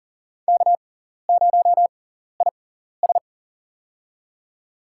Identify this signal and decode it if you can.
Morse code, heard as a keyed tone, about 37 wpm, spaced as for 13 wpm, 700 Hz. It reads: K0IS
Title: USA Call Signs - 6x Character Spacing - 40wpm